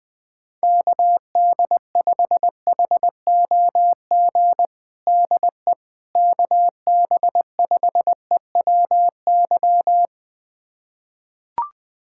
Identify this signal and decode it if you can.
Morse code, heard as a keyed tone, about 20 wpm, 700 Hz. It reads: KD5HOG DE KB5EWY
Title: USA Callsign DE Callsign - 6 Characters - 20wpm